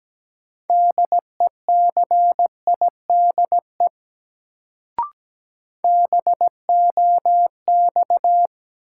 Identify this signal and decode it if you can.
Morse code, heard as a keyed tone, about 17 wpm, 700 Hz. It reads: DECIDE BOX
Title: Sentences from Top 500 Words - New Word Drill - 17wpm